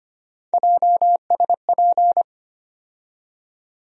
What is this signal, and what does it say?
Morse code, heard as a keyed tone, about 25 wpm, 700 Hz. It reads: JSP
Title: USA Call Sign Postfixes - 3 Letters 25wpm